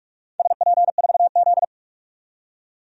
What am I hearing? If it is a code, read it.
Morse code, heard as a keyed tone, about 44 wpm, 700 Hz. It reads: SP4Z
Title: DXCC-100 Call Signs - 4 Characters - 45wpm